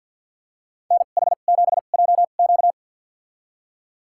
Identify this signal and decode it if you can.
Morse code, heard as a keyed tone, about 52 wpm, spaced as for 37 wpm, 690 Hz. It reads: NH7JQ
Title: USA Call Signs - 5 Characters - 2x Character Spacing - 50wpm